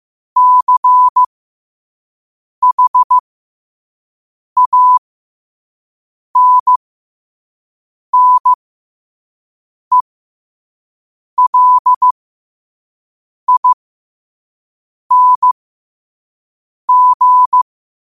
Morse code, heard as a keyed tone, about 15 wpm, spaced as for 5 wpm, 1000 Hz. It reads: CHANNELING
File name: qotdfiles / 5wpm